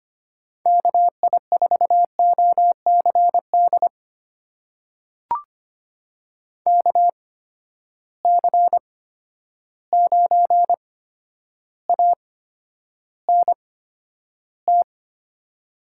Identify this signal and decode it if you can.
Morse code, heard as a keyed tone, about 25 wpm, spaced as for 11 wpm, 700 Hz. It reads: KI4OCD KC9ANT
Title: USA Call Signs - 6 Characters - 8x Character Spacing - 25wpm